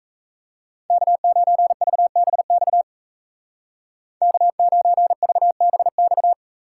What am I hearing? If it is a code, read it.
Morse code, heard as a keyed tone, about 34 wpm, 700 Hz. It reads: K9VBX K9VBX
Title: Speed Racing - USA 1x3 Call Signs 28wpm